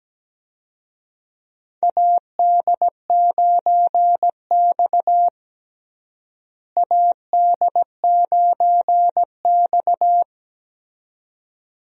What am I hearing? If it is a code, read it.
Morse code, heard as a keyed tone, about 17 wpm, 700 Hz. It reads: AD9X AD9X
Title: USA Call Signs - 4 Characters - Repeated 2x - 17wpm